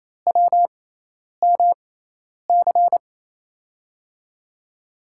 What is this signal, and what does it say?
Morse code, heard as a keyed tone, about 28 wpm, spaced as for 10 wpm, 700 Hz. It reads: WMC
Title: USA Call Signs - 5 Characters - 6x Character Spacing - 28wpm